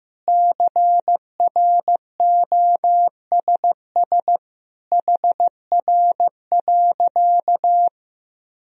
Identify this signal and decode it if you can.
Morse code, heard as a keyed tone, about 14 wpm, 700 Hz. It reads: CROSS HR.
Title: Sentences from Top 400 Words and Abbreviations 15wpm